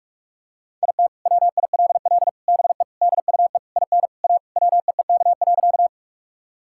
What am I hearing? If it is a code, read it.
Morse code, heard as a keyed tone, about 42 wpm, 700 Hz. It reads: IT WILL BE DUE IN A WEEK.
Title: Sentences from Top 1000 Words - Review Part 6 - 45wpm